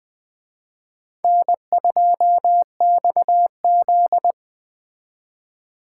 Morse code, heard as a keyed tone, about 20 wpm, 700 Hz. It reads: N2XZ